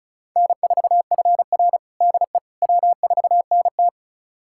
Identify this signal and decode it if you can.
Morse code, heard as a keyed tone, about 36 wpm, 700 Hz. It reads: N4FR DE W4NT
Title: USA Callsign DE Callsign - 4 Characters - 35wpm